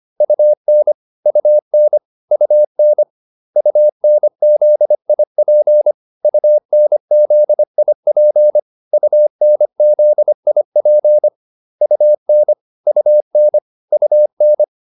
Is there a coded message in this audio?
Yes: Morse code from 0.2 to 14.6 s, heard as a keyed tone, about 25 wpm, 600 Hz. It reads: UN UN UN UNZIP UNZIP UNZIP UN UN UN